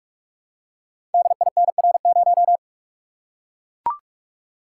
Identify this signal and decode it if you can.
Morse code, heard as a keyed tone, about 44 wpm, 700 Hz. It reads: DINR0
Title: Code Groups - 5 Letter:Numbers - 45wpm